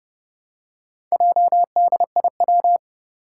Morse code, heard as a keyed tone, about 30 wpm, 700 Hz. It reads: JDIW
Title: Four Challenging Letter-Number Combinations 30wpm